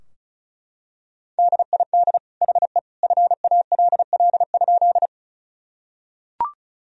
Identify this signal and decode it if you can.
Morse code, heard as a keyed tone, about 34 wpm, 700 Hz. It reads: DID HE FALL?
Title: Sentences from Top 1000 Words - Review Part 2 - 35wpm